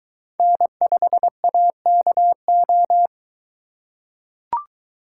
Morse code, heard as a keyed tone, about 21 wpm, 700 Hz. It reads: N5AKO